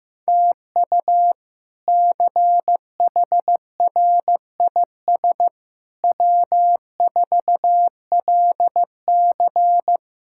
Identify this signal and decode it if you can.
Morse code, heard as a keyed tone, about 15 wpm, 700 Hz. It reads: TU CHRIS W4LC
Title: K1USN SST QSO Practice - 15wpm